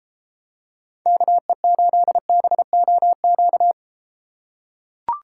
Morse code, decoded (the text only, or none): KE8BOQ